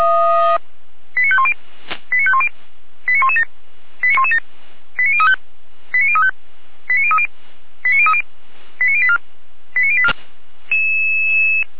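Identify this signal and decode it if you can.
Five-tone selective call (ZVEI), heard as quick runs of five tones, tones of about 70 ms. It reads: 89510 89510 89117 89117 89035 89035 89030 89030 89084 89084
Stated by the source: Five-tone ZVEI selective call 89079, received on SDR.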